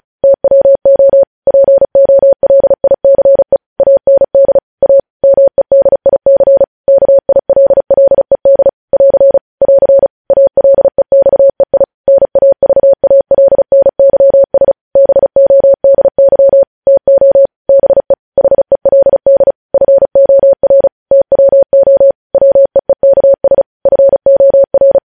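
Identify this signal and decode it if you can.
Morse code, heard as a keyed tone, about 36 wpm, 550 Hz. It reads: TWO POLICE AND A MEDIC KILLED <AR> <AR> ALEXEI NAVALNYS BODY TO BE HELD FOR TWO WEEKS FOR
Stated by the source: Morse code news headlines sent at 35, beginning with vvv.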